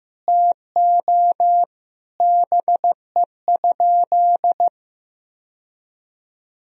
Morse code, heard as a keyed tone, about 15 wpm, 700 Hz. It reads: TO BE?
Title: Sentences from Top 1000 Words - Review Part 2 - 15wpm